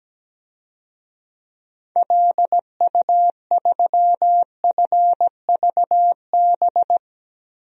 Morse code, heard as a keyed tone, about 17 wpm, 700 Hz. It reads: LU3FVB
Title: DXCC-100 Call Signs - 6 Characters - Repeated 3x - 17wpm